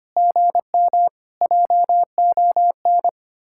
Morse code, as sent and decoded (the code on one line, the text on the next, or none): --. -- / .--- --- -.
GM JON